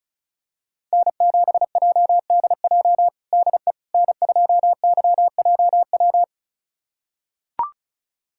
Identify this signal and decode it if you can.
Morse code, heard as a keyed tone, about 34 wpm, 700 Hz. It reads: N7JDJ DE N2YJW
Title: USA Callsign DE Callsign 35wpm